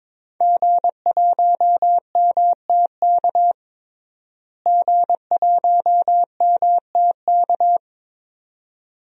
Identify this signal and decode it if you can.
Morse code, heard as a keyed tone, about 22 wpm, 700 Hz. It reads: G1MTK G1MTK
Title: DXCC-100 Call Signs - 5 Characters - Repeated 2x - 22wpm